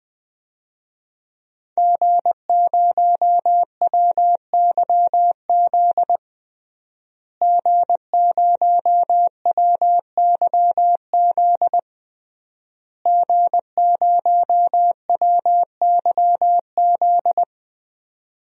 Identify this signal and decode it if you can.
Morse code, heard as a keyed tone, about 21 wpm, 700 Hz. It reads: G0WYZ G0WYZ G0WYZ
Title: DXCC-100 Call Signs - 5 Characters - Repeated 3x - 20wpm